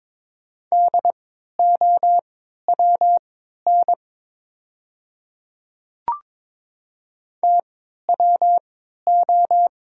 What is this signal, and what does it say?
Morse code, heard as a keyed tone, about 22 wpm, spaced as for 13 wpm, 690 Hz. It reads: DOWN TWO
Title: 3x Character Spacing - Top 100 Words - 22wpm